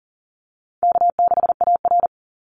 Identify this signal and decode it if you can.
Morse code, heard as a keyed tone, about 40 wpm, 700 Hz. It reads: K6AR
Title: Call Signs - 40wpm